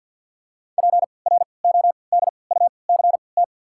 Morse code, heard as a keyed tone, about 53 wpm, spaced as for 29 wpm, 690 Hz. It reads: PRODUCT
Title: 3x Character Spacing - Top 500 Words - 50wpm